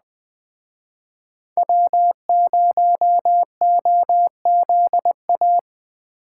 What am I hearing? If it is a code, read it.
Morse code, heard as a keyed tone, about 20 wpm, 700 Hz. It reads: W0OZA